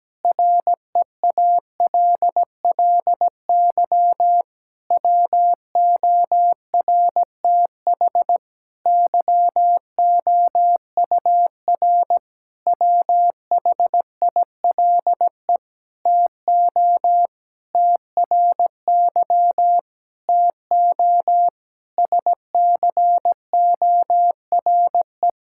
Morse code, heard as a keyed tone, about 17 wpm, 700 Hz. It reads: REALLY WORTH YOUR WHILE TO TRY TO SCORE
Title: Sentences from Top 1600 Words 17wpm